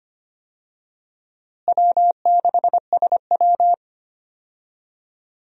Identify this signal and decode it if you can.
Morse code, heard as a keyed tone, about 25 wpm, 700 Hz. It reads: W6SW
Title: USA Call Signs - 4 Characters - Repeated 2x - 25wpm